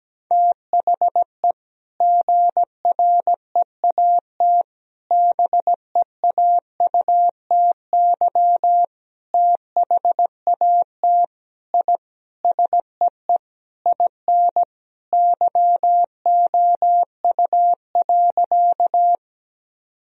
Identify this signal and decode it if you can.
Morse code, heard as a keyed tone, about 17 wpm, 700 Hz. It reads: THE GREAT BEAUTY THAT I SEE IN YOU.
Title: Sentences from Top 1000 Words - Review Part 2 - 17wpm